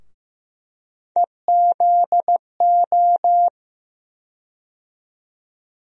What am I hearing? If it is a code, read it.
Morse code, heard as a keyed tone, about 15 wpm, 700 Hz. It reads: EZO
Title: USA Call Sign Postfixes - 3 Letters 15wpm